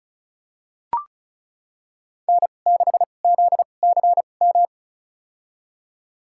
Morse code, heard as a keyed tone, about 34 wpm, spaced as for 25 wpm, 700 Hz. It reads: N6ZCM
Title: USA Call Signs - 5 Characters - 2x Character Spacing - 35wpm